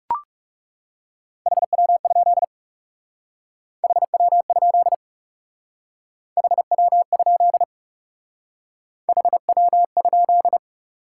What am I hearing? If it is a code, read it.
Morse code, heard as a keyed tone, about 38 wpm, 700 Hz. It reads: HW? HW? HW? HW?